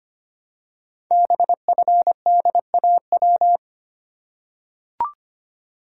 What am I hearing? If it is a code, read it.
Morse code, heard as a keyed tone, about 25 wpm, 700 Hz. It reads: BFDAW
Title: Code Groups - 5 Letters - 25wpm